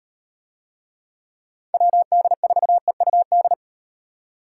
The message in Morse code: WD4EUD

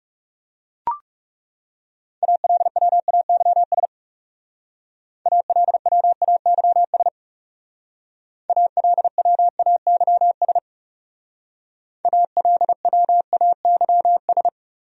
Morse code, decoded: ALWAYS ALWAYS ALWAYS ALWAYS

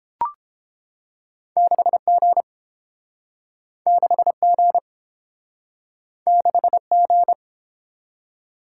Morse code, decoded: 6G 6G 6G

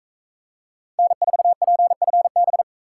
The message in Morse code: N4PFB

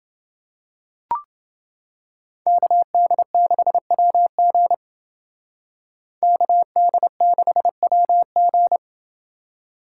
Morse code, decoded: KD6WG KD6WG

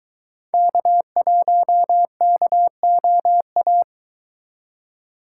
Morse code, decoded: K1KOA

